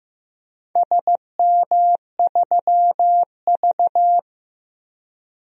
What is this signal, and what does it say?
Morse code, heard as a keyed tone, about 15 wpm, 700 Hz. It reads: SM3V
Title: DXCC-100 Call Signs - 4 Characters - Repeated 2x - 15wpm